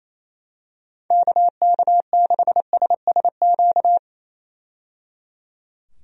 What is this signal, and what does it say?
Morse code, heard as a keyed tone, about 28 wpm, 700 Hz. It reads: KK6SSQ
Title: USA Call Signs - 6 Characters - 28wpm